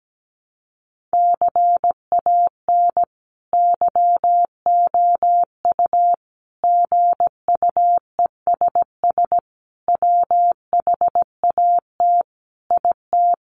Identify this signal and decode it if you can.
Morse code, heard as a keyed tone, about 17 wpm, 700 Hz. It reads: CAN YOU GUESS WHAT IT
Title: Sentences from Top 1500 Words 17wpm